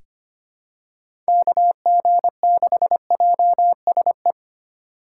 Morse code, heard as a keyed tone, about 25 wpm, 700 Hz. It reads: KG6JSE
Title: USA Call Signs - 6 Characters - 4x Character Spacing - 25wpm